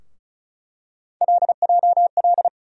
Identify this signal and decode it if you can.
Morse code, heard as a keyed tone, about 36 wpm, 700 Hz. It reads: LJL
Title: USA Call Sign Postfixes - 3 Letters 35wpm